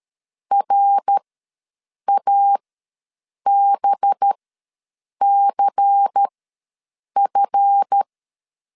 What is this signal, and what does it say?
Morse code, heard as a keyed tone, about 13 wpm, spaced as for 7 wpm, 800 Hz. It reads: RABCF
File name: LEZIONI / 9 2867 ws 12wpm 3431